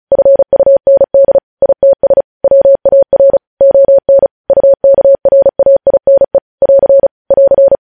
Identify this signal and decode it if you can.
Morse code, heard as a keyed tone, about 36 wpm, 550 Hz. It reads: FUND ITS WAR ON UKRAINE <AR> <AR>
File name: podcast / News headlines-1x-35x35